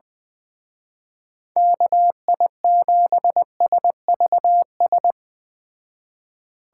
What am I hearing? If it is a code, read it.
Morse code, heard as a keyed tone, about 19 wpm, 700 Hz. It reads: KI7SVS
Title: Speed Racing - USA 2x3 Call Signs 17wpm